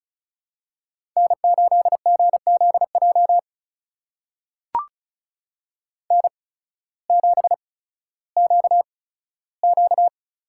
Morse code, heard as a keyed tone, about 36 wpm, spaced as for 16 wpm, 700 Hz. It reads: N8GZJ N7QQ
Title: USA Call Signs - 5 Characters - 8x Character Spacing - 35wpm